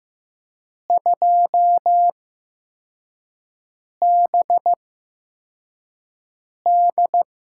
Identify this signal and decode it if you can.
Morse code, heard as a keyed tone, about 15 wpm, spaced as for 4 wpm, 700 Hz. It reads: 2BD
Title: USA Call Signs - 5 Characters - 8x Character Spacing - 15wpm